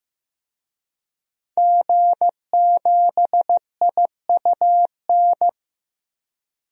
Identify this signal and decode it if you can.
Morse code, heard as a keyed tone, about 15 wpm, 700 Hz. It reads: G7IUN
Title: DXCC-100 Call Signs - 5 Characters - 15wpm